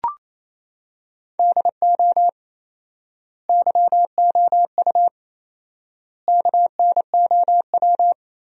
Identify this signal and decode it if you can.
Morse code, heard as a keyed tone, about 28 wpm, 700 Hz. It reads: DO YOU KNOW
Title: 4x Word-Spacing - Sentences from Top 200 Words - 28wpm